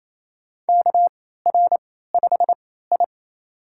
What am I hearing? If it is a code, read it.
Morse code, heard as a keyed tone, about 28 wpm, spaced as for 16 wpm, 700 Hz. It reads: KR5I
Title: USA Call Signs - 4 Characters - 3x Character Spacing - 28wpm